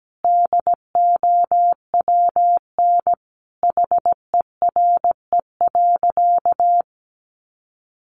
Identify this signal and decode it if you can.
Morse code, heard as a keyed tone, about 17 wpm, 700 Hz. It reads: DOWN HERE.
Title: Sentences from Top 1100 Words 17wpm